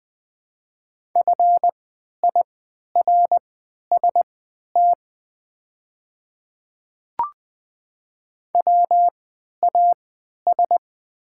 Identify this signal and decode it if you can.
Morse code, heard as a keyed tone, about 20 wpm, spaced as for 11 wpm, 700 Hz. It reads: FIRST WAS
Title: 3x Character Spacing - Top 100 Words - 20wpm